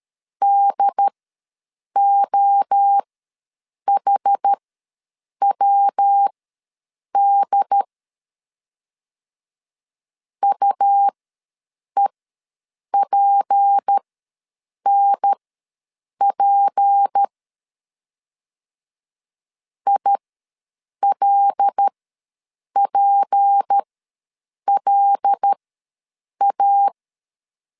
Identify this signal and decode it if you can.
Morse code, heard as a keyed tone, about 13 wpm, spaced as for 7 wpm, 800 Hz. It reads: DOHWD UEPNP ILPLA
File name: LEZIONI / 6 PKX0 / 6 P ws 12wpm 3117